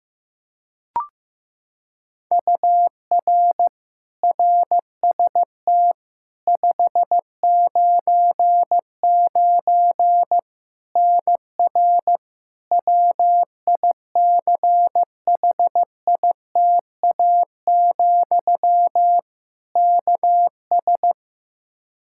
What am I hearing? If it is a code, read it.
Morse code, heard as a keyed tone, about 15 wpm, 700 Hz. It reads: UR RST 599 NR WICHITA, KS
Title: K1USN SST QSO Practice 15wpm